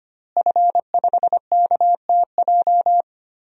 Morse code, heard as a keyed tone, about 25 wpm, 700 Hz. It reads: F5KTJ
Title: DXCC-100 Call Signs - 5 Characters - Repeated 3x - 25wpm